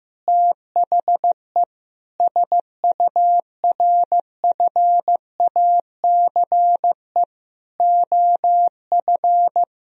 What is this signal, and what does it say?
Morse code, heard as a keyed tone, about 15 wpm, 700 Hz. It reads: THE SURFACE OF
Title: Sentences from Top 1000 Words - Review Part 3 - 15wpm